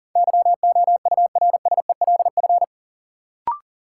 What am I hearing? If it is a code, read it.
Morse code, heard as a keyed tone, about 40 wpm, 700 Hz. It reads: YOURSELF
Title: Sentences from Top 600 Words - New Word Drill - 40wpm